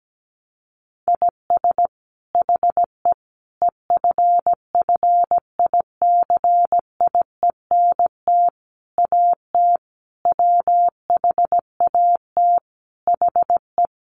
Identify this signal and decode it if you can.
Morse code, heard as a keyed tone, about 17 wpm, 700 Hz. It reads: IS HE EFFICIENT AT WHAT HE
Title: Sentences from Top 1300 Words 17wpm